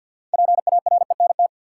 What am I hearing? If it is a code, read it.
Morse code, heard as a keyed tone, about 47 wpm, 700 Hz. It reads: PARENT
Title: Six Letter Words - 50wpm